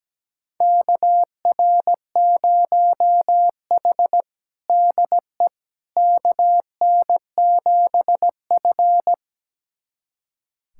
Morse code, heard as a keyed tone, about 17 wpm, 700 Hz. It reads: KR0H DE KN7F
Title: USA Callsign DE Callsign - 4 Characters - 17wpm